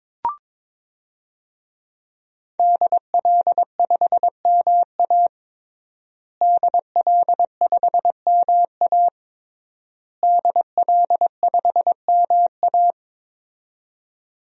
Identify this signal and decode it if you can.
Morse code, heard as a keyed tone, about 21 wpm, 700 Hz. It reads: DL5MA DL5MA DL5MA
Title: DXCC-100 Call Signs - 5 Characters - Repeated 3x - 22wpm